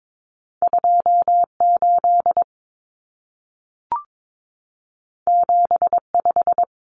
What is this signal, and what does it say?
Morse code, heard as a keyed tone, about 22 wpm, 700 Hz. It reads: 28 75
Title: Common Cut Numbers - 2-Digit 22wpm